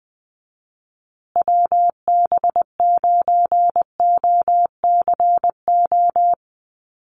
Morse code, heard as a keyed tone, about 20 wpm, 700 Hz. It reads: WB9OCO
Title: USA Call Signs Repeated 3x 20wpm